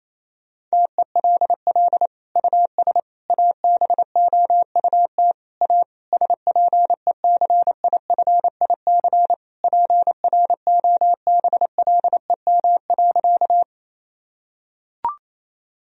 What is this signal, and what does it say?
Morse code, heard as a keyed tone, about 28 wpm, 700 Hz. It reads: TELL US ABOUT A SPECIFIC PROBLEM.
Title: Sentences from Top 1000 Words - Review Part 3 - 28wpm